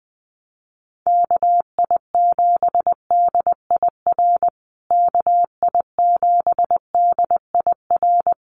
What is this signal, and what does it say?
Morse code, heard as a keyed tone, about 20 wpm, 700 Hz. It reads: KI7DIR KI7DIR
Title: USA Call Signs Repeated 3x 20wpm